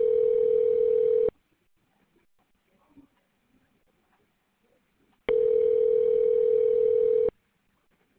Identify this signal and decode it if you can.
Ringback tone (North America), 440 + 480 Hz.